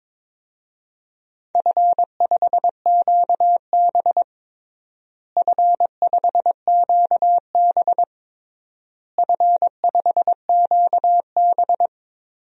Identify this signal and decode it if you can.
Morse code, heard as a keyed tone, about 22 wpm, 700 Hz. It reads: F5QB F5QB F5QB
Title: DXCC-100 Call Signs - 4 Characters - Repeated 3x - 22wpm